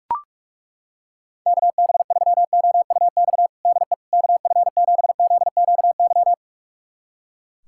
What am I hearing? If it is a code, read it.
Morse code, heard as a keyed tone, about 41 wpm, 700 Hz. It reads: KB3OUX DE KF7ZQY
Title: USA Callsign DE Callsign - 6 Characters - 45wpm